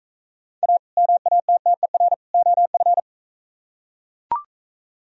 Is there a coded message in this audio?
Yes: Morse code from 0.6 to 3.0 s, heard as a keyed tone, about 43 wpm, 700 Hz. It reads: A MATTER OF